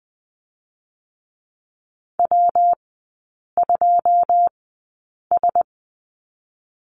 Morse code, heard as a keyed tone, about 19 wpm, 700 Hz. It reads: W 2 S
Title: 114 Lesson 19e - Letter w - Three Characters 20wpm